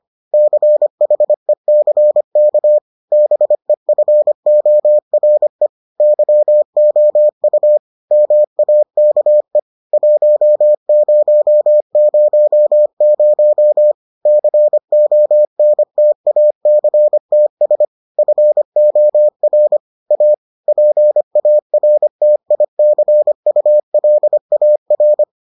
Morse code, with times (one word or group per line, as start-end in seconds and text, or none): 0.3-2.8 CHECK
3.1-5.7 BEFORE
6.0-7.8 YOU
8.1-9.6 MAKE
9.9-13.9 1000
14.3-17.8 CONTACTS
18.2-19.8 FOR
20.1-20.3 A
20.7-25.2 PARTICULAR